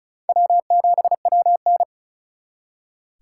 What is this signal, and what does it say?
Morse code, heard as a keyed tone, about 45 wpm, 700 Hz. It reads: W7WN